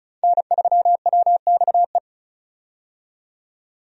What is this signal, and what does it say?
Morse code, heard as a keyed tone, about 34 wpm, 700 Hz. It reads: N3WXE
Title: USA Call Signs - 5 Characters - 35wpm